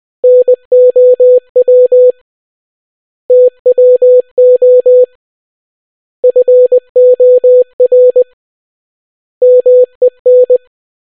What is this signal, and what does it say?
Morse code, heard as a keyed tone, about 20 wpm, 500 Hz. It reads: NOW TWO FOR MEN